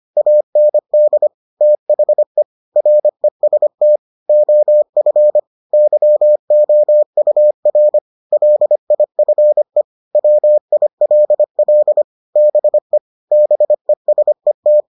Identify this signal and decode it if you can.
Morse code, heard as a keyed tone, about 25 wpm, 600 Hz. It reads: AND THE REST OF YOUR LIFE WILL BE BESET